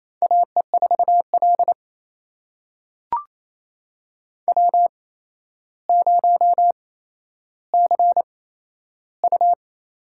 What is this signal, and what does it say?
Morse code, heard as a keyed tone, about 28 wpm, spaced as for 12 wpm, 700 Hz. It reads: AE4L W0CU
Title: USA Call Signs - 4 Characters - 8x Character Spacing - 28wpm